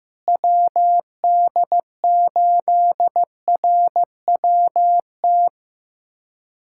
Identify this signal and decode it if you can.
Morse code, heard as a keyed tone, about 15 wpm, 700 Hz. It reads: WD8RWT
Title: USA Callsign DE Callsign 15wpm